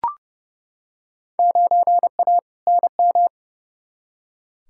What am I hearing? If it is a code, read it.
Morse code, heard as a keyed tone, about 32 wpm, 700 Hz. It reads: 9A NM
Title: Field Day Exchanges Encore - 30wpm